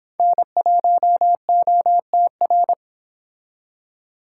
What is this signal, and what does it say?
Morse code, heard as a keyed tone, about 26 wpm, 700 Hz. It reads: N1OTR